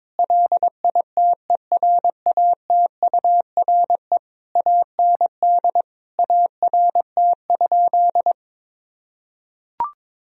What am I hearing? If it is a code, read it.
Morse code, heard as a keyed tone, about 22 wpm, 700 Hz. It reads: LITERATURE AND ART?
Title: Sentences from Top 1000 Words - Review Part 3 - 22wpm